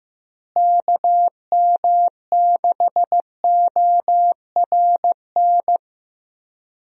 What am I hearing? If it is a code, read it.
Morse code, heard as a keyed tone, about 15 wpm, 700 Hz. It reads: KM6ORN
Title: USA Call Signs - 6 Characters - Repeated 2x - 15wpm